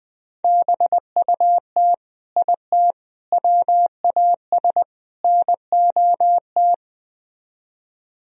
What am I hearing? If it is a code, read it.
Morse code, heard as a keyed tone, about 21 wpm, 700 Hz. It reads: BUT IT WAS NOT